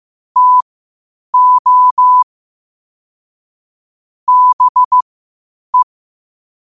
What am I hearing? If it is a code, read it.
Morse code, heard as a keyed tone, about 15 wpm, spaced as for 8 wpm, 1000 Hz. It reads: TO BE
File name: qotdfiles / 8wpm